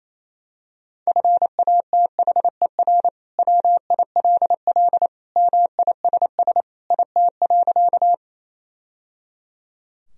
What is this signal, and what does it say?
Morse code, heard as a keyed tone, about 28 wpm, 700 Hz. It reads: FATHER WILL MISS IT.
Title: Sentences from Top 1000 Words - Review Part 3 - 28wpm